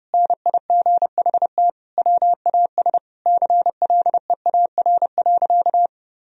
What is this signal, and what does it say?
Morse code, heard as a keyed tone, about 32 wpm, 700 Hz. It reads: NIGHT WAS CLEAR.
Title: Sentences from Top 1000 Words - Review Part 1 - 30wpm